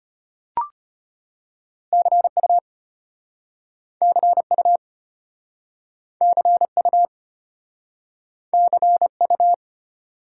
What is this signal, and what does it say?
Morse code, heard as a keyed tone, about 32 wpm, 700 Hz. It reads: CU CU CU CU